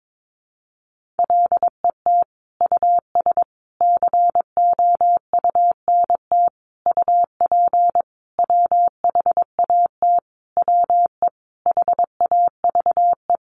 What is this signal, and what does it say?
Morse code, heard as a keyed tone, about 22 wpm, 700 Hz. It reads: LET US COUNT UP WHAT WE HAVE
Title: Sentences from Top 1500 Words 22wpm